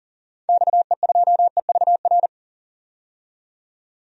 Morse code, heard as a keyed tone, about 40 wpm, 700 Hz. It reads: XE2EVR
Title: DXCC-100 Call Signs - 6 Characters - 40wpm